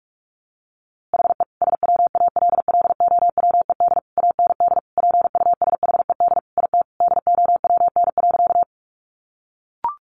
Morse code, heard as a keyed tone, about 44 wpm, 700 Hz. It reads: HE SWALLOWED AND PUSHED IT DOWN.
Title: A Sound Of Thunder 45wpm